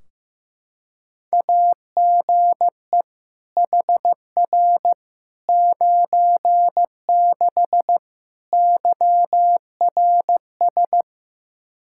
Morse code, heard as a keyed tone, about 15 wpm, 700 Hz. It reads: AGE HR 96 YRS
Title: K1USN SST QSO Practice 15wpm